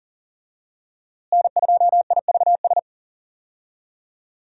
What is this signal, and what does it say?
Morse code, heard as a keyed tone, about 40 wpm, 700 Hz. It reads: N2IVS
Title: USA Call Signs - 5 Characters - Repeated 2x - 40wpm